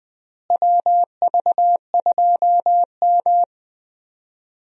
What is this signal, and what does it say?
Morse code, heard as a keyed tone, about 20 wpm, 700 Hz. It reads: WV2M
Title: USA Call Signs - 4 Characters - Repeated 2x - 20wpm